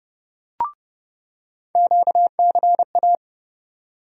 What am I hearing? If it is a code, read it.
Morse code, heard as a keyed tone, about 29 wpm, 700 Hz. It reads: QCA
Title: USA Call Sign Postfixes - 3 Letters 30wpm